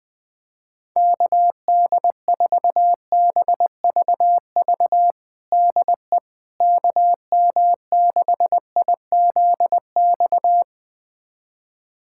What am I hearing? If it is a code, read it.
Morse code, heard as a keyed tone, about 21 wpm, 700 Hz. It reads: KD4BVV DE KM6IZX
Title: USA Callsign DE Callsign - 6 Characters - 20wpm